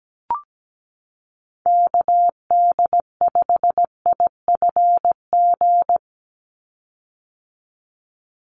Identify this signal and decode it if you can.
Morse code, heard as a keyed tone, about 17 wpm, 700 Hz. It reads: KD5IFG
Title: USA Call Signs 17wpm